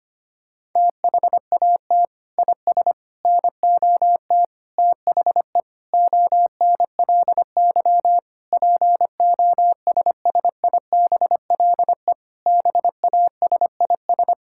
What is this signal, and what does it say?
Morse code, heard as a keyed tone, about 25 wpm, 700 Hz. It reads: THAT IS NOT THE ONLY POSSIBLE BASIS
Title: Sentences from Top 1000 Words - Review Part 5 - 25wpm